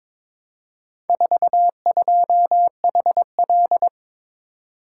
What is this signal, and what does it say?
Morse code, heard as a keyed tone, about 22 wpm, 700 Hz. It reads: 42HL
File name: practice / 22wpm / Four Letter-Number Combinations - 22wpm